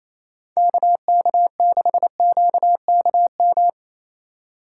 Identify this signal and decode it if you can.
Morse code, heard as a keyed tone, about 27 wpm, 700 Hz. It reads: KK6QKM
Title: USA Call Signs - 6 Characters - 4x Character Spacing - 28wpm